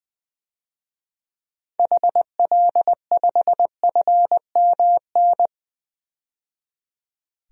Morse code, heard as a keyed tone, about 20 wpm, 700 Hz. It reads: HL5FMN
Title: DXCC-100 Call Signs - 6 Characters - 20wpm